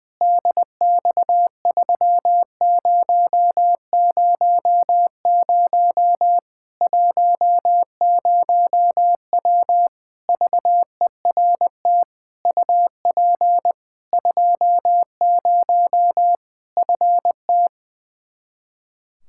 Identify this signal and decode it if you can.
Morse code, heard as a keyed tone, about 20 wpm, 700 Hz. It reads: DX3000 10W VERT UP 20 FT
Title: Practice QSO - Exchange 2 - Info - 20wpm